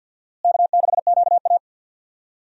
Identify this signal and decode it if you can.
Morse code, heard as a keyed tone, about 48 wpm, 700 Hz. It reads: K6QA
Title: USA Call Signs - 4 Characters - Repeated 2x - 50wpm